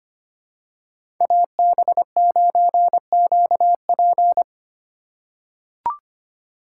Morse code, heard as a keyed tone, about 25 wpm, 700 Hz. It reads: AB9QP
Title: USA Call Signs - 5 Characters - 4x Character Spacing - 25wpm